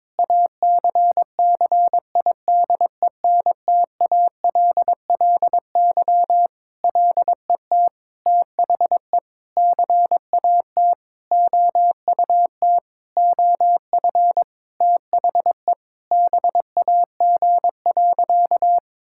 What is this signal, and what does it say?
Morse code, heard as a keyed tone, about 22 wpm, 700 Hz. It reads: ACCIDENTALLY LET THE CAT OUT OF THE BAG.
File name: Sentences with 7-Word Phrases - 22wpm